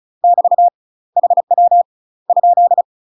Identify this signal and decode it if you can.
Morse code, heard as a keyed tone, about 34 wpm, 700 Hz. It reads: <BT> HW ?
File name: UK MASTER QSOs TRAINING / 35WPM / UK MASTER QSOs 0005